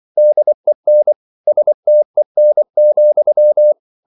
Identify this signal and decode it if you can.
Morse code, heard as a keyed tone, about 24 wpm, 600 Hz. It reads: DEN STEN,
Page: Ferie 24 wpm